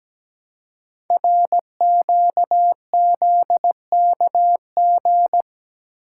Morse code, heard as a keyed tone, about 17 wpm, 700 Hz. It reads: RQZKG
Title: Code Groups - 5 Letter:Numbers - 17wpm